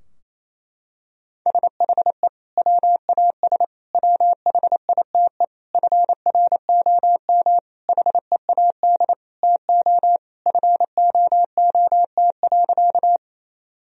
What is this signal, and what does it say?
Morse code, heard as a keyed tone, about 28 wpm, 700 Hz. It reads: SHE WAS WHITE FROM HEAD TO FOOT.
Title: Sentences from Top 1000 Words - Review Part 3 - 28wpm